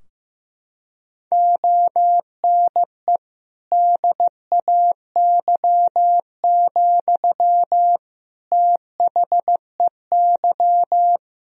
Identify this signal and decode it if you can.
Morse code, heard as a keyed tone, about 15 wpm, 700 Hz. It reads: ONE DAY, THEY
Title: Sentences from Top 1000 Words - Review Part 2 - 15wpm